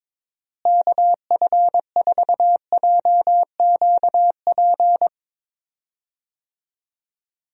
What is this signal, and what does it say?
Morse code, heard as a keyed tone, about 23 wpm, 700 Hz. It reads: KF4JQP
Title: USA Call Signs - 6 Characters - Repeated 2x - 22wpm